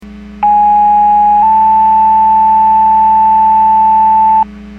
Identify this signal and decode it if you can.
Two-tone page, heard as a one-second tone, then a three-second tone, about 830 then 880 Hz. It is heard from 0.4 to 4.4 s.
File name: Portage 154.385 Paging